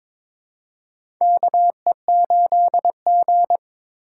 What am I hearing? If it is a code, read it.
Morse code, heard as a keyed tone, about 21 wpm, 700 Hz. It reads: KE8G
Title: USA Call Signs - 4x Character Spacing - 22wpm